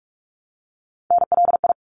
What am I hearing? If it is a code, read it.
Morse code, heard as a keyed tone, about 46 wpm, 700 Hz. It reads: NLI
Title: ARRL Sections 45wpm